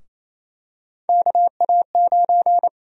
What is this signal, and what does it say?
Morse code, heard as a keyed tone, about 27 wpm, 700 Hz. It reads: KA9
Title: USA Call Sign Prefixes - 28wpm